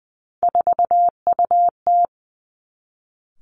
Morse code, heard as a keyed tone, about 20 wpm, 700 Hz. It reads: 4UT
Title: Three Challenging Letter-Number Combinations 20wpm